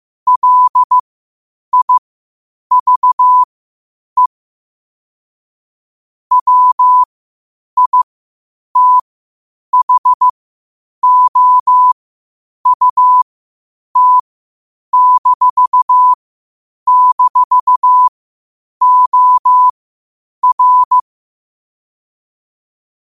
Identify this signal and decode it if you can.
Morse code, heard as a keyed tone, about 15 wpm, spaced as for 8 wpm, 1000 Hz. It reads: LIVE WITHOUT--OR
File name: qotdfiles / 8wpm